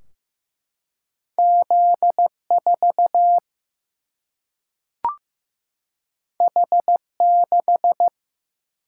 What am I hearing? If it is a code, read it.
Morse code, heard as a keyed tone, about 16 wpm, 700 Hz. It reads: Z 4 H 6
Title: Single-Letter Single-Number 15wpm